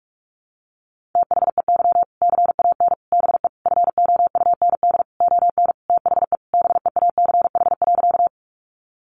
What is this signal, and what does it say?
Morse code, heard as a keyed tone, about 44 wpm, 700 Hz. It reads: THEY CAN BE FOUND ON THE BEACH.
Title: Sentences from Top 1500 Words 45wpm